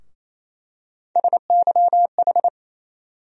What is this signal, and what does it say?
Morse code, heard as a keyed tone, about 28 wpm, 700 Hz. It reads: SYH
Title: USA Call Sign Postfixes - 3 Letters 28wpm